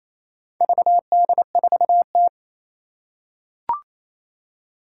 Morse code, heard as a keyed tone, about 27 wpm, 700 Hz. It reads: VD4T
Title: Four Challenging Letter-Number Combinations 28wpm